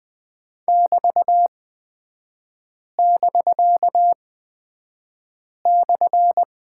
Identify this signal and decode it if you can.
Morse code, heard as a keyed tone, about 20 wpm, spaced as for 5 wpm, 700 Hz. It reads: <BT><BK>/
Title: 250 Lesson 40e - BK - Three Characters 20wpm